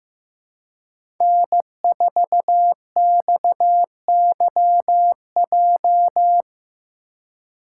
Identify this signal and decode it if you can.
Morse code, heard as a keyed tone, about 15 wpm, 700 Hz. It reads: N4XYJ
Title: USA Call Signs - 4x Character Spacing - 15wpm